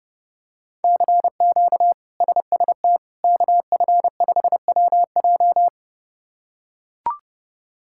Morse code, heard as a keyed tone, about 30 wpm, 700 Hz. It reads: CQ SST KF5WJ
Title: K1USN SST QSO Practice - 30wpm